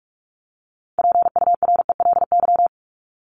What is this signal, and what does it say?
Morse code, heard as a keyed tone, about 44 wpm, 700 Hz. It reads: PURELY
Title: LY Suffix - Top 100 Words with Base Words 45wpm